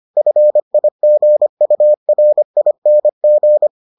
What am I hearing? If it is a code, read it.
Morse code, heard as a keyed tone, about 25 wpm, 600 Hz. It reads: FIGURING